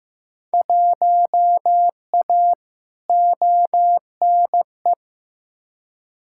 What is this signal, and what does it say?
Morse code, heard as a keyed tone, about 15 wpm, 700 Hz. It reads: 1A ONE
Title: Field Day Exchanges - 15wpm